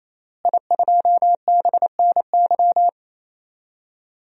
Morse code, heard as a keyed tone, about 28 wpm, 700 Hz. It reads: I2BNY
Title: DXCC-100 Call Signs - 5 Characters - Repeated 3x - 28wpm